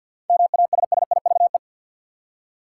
Morse code, heard as a keyed tone, about 52 wpm, 700 Hz. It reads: MASSIVE
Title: Seven Letter Words - 50wpm